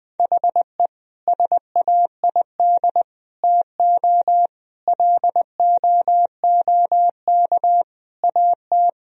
Morse code, heard as a keyed tone, about 21 wpm, 700 Hz. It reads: HE SAID TO LOOK AT
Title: Sentences from Top 1000 Words - Review Part 1 - 20wpm